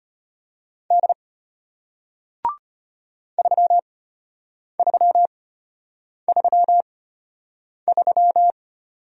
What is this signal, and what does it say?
Morse code, heard as a keyed tone, about 33 wpm, spaced as for 8 wpm, 700 Hz. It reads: D 3333